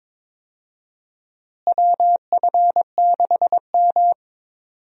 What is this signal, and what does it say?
Morse code, heard as a keyed tone, about 22 wpm, 700 Hz. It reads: WF6M
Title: USA Call Signs - 4 Characters - Repeated 3x - 22wpm